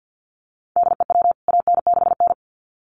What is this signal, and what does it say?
Morse code, heard as a keyed tone, about 50 wpm, 700 Hz. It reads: DEW AN6N
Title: CWT Exchange 50wpm